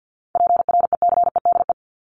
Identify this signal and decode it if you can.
Morse code, heard as a keyed tone, about 50 wpm, 700 Hz. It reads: PRECEDE
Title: ING Suffix - Top 100 Words with Base Words 50wpm